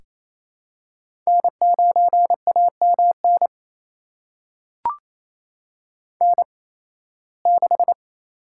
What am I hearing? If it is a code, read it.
Morse code, heard as a keyed tone, about 28 wpm, spaced as for 18 wpm, 700 Hz. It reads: N9AMN N6
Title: USA Call Signs - 8x Character Spacing - 28wpm